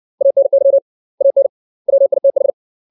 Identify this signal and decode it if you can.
Morse code, heard as a keyed tone, about 60 wpm, 550 Hz. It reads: ANY AN WITH
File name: 632 words / 632 Words 60WPM